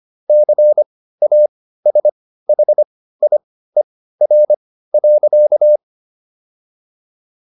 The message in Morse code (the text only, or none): CASHIER.